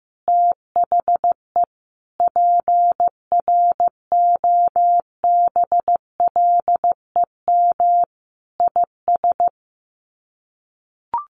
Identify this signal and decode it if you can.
Morse code, heard as a keyed tone, about 15 wpm, 700 Hz. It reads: THE PROBLEM IS
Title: Sets of 5 Words Encore2 15wpm